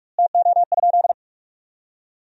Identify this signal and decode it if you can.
Morse code, heard as a keyed tone, about 42 wpm, 700 Hz. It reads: TO?
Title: 2x Word-Spacing - Sentences from Top 100 Words - 45wpm